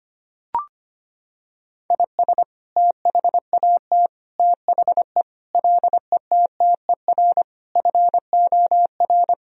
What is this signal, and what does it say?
Morse code, heard as a keyed tone, about 25 wpm, 700 Hz. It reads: IS THAT THE LETTER FOR
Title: Sentences from Top 1000 Words - Review Part 2 - 25wpm